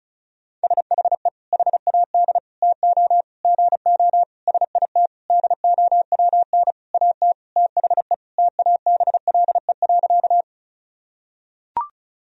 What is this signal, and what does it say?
Morse code, heard as a keyed tone, about 34 wpm, 700 Hz. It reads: SHE HAD TO GO SIT DOWN AT THE TABLE.
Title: Sentences from Top 1000 Words - Review Part 2 - 35wpm